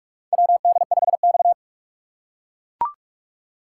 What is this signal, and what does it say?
Morse code, heard as a keyed tone, about 44 wpm, 700 Hz. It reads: WD5X
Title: USA Call Signs - 4 Characters - 3x Character Spacing - 45wpm